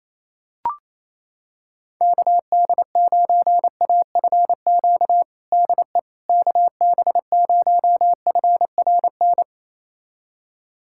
Morse code, heard as a keyed tone, about 28 wpm, 700 Hz. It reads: KD9AFQ DE KB0FRN
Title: USA Callsign DE Callsign - 6 Characters - 28wpm